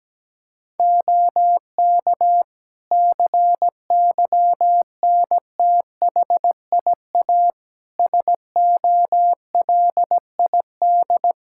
Morse code, heard as a keyed tone, about 17 wpm, 700 Hz. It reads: OK CYNTHIA SOLID
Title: Practice QSO - Exchange 3 - More Info - 17wpm